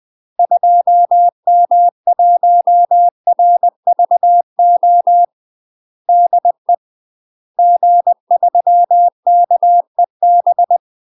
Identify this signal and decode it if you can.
Morse code, heard as a keyed tone, about 20 wpm, 700 Hz. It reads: 2M1RVO DE G3KEB
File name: UK MASTER QSOs TRAINING/20WPM